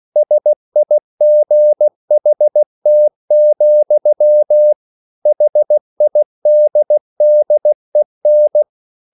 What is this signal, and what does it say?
Morse code, heard as a keyed tone, about 16 wpm, 600 Hz. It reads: SIGHT, HIDDEN